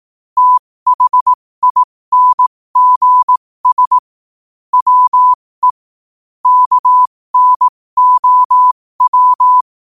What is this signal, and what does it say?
Morse code, heard as a keyed tone, about 18 wpm, spaced as for 15 wpm, 1000 Hz. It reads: THINGS WE KNOW
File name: qotdfiles / 15wpm